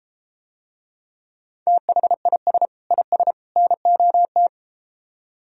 Morse code, heard as a keyed tone, about 32 wpm, 700 Hz. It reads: THIS IS NOT